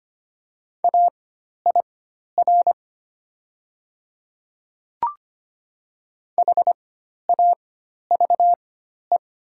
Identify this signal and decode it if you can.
Morse code, heard as a keyed tone, about 25 wpm, spaced as for 12 wpm, 700 Hz. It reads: AIR HAVE
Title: 4x Character Spacing - Top 300 Words - 25wpm